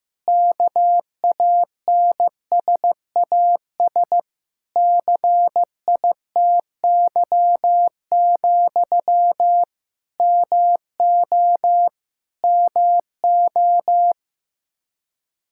Morse code, heard as a keyed tone, about 15 wpm, 700 Hz. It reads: KANSAS CITY, MO MO
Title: K1USN SST QSO Practice 15wpm